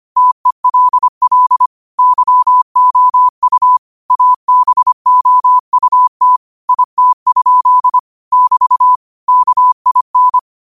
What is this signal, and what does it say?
Morse code, heard as a keyed tone, about 25 wpm, 1000 Hz. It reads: TELL YOU ABOUT IT? <BT> KIN
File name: qotdfiles / 25wpm